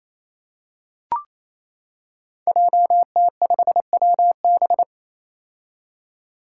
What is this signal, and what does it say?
Morse code, heard as a keyed tone, about 28 wpm, 700 Hz. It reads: JT5WB
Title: Five Challenging Letter-Number Combinations 28wpm